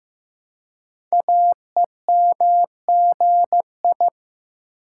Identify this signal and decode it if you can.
Morse code, heard as a keyed tone, about 15 wpm, 700 Hz. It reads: AEMGI
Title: Code Groups - 5 Letters - 15wpm